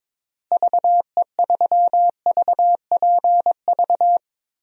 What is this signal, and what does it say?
Morse code, heard as a keyed tone, about 22 wpm, 700 Hz. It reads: VE3VPV
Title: DXCC-100 Call Signs - 6 Characters - Repeated 3x - 22wpm